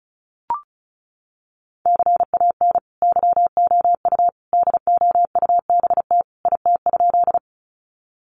Morse code, heard as a keyed tone, about 34 wpm, 700 Hz. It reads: CAN YOU DOUBT IT?
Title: Sentences from Top 1500 Words 35wpm